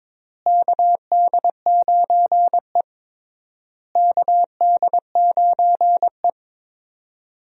Morse code, heard as a keyed tone, about 22 wpm, 700 Hz. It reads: KD9E KD9E
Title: USA Call Signs - 4 Characters - Repeated 3x - 22wpm